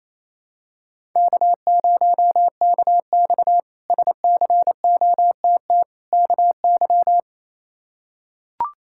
Morse code, heard as a keyed tone, about 28 wpm, 700 Hz. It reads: K0KX SCOTT KY
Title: K1USN SST QSO Practice - 28wpm